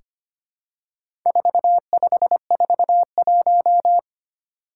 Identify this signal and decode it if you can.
Morse code, heard as a keyed tone, about 25 wpm, 700 Hz. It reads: 4541
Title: Four Challenging Letter-Number Combinations 25wpm